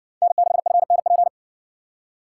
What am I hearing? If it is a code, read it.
Morse code, heard as a keyed tone, about 59 wpm, 700 Hz. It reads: N6FNP